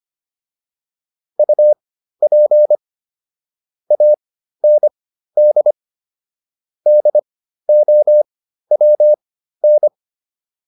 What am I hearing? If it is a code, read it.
Morse code, heard as a keyed tone, about 25 wpm, spaced as for 13 wpm, 600 Hz. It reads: UP AND DOWN